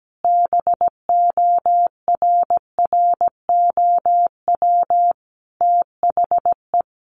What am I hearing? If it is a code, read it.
Morse code, heard as a keyed tone, about 17 wpm, 700 Hz. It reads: BORROW THE
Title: Sentences from Top 1500 Words 17wpm